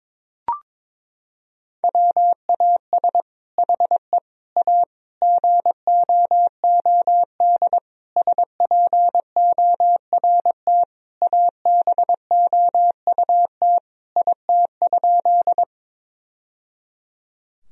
Morse code, heard as a keyed tone, about 21 wpm, 700 Hz. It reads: WAS HE A GOOD SPORT ABOUT IT?
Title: Sentences from Top 1000 Words - Review Part 4 - 22wpm